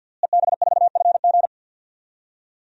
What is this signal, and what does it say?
Morse code, heard as a keyed tone, about 50 wpm, 700 Hz. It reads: EB4FG
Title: DXCC-100 Call Signs - 5 Characters - Repeated 3x - 50wpm